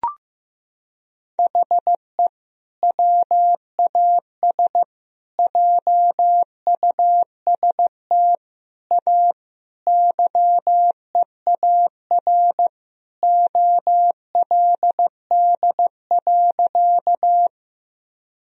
Morse code, heard as a keyed tone, about 15 wpm, 700 Hz. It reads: HE WAS JUST A YEAR OLD.
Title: Sentences from Top 1000 Words - Review Part 1 - 15wpm